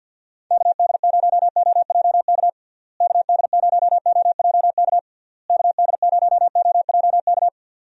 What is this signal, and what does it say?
Morse code, heard as a keyed tone, about 46 wpm, 700 Hz. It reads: KD0OJK KD0OJK KD0OJK
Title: USA Call Signs - 6 Characters - Repeated 3x - 50wpm